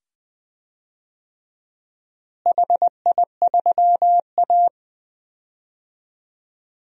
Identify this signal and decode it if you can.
Morse code, heard as a keyed tone, about 21 wpm, 700 Hz. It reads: HI3A
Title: DXCC-100 Call Signs - 4 Characters - Repeated 2x - 20wpm